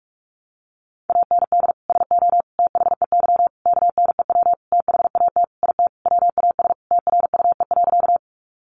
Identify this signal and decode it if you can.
Morse code, heard as a keyed tone, about 44 wpm, 700 Hz. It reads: AND SO THEY KNEW THAT IT WAS TRUE.
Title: Sentences from Top 400 Words 45wpm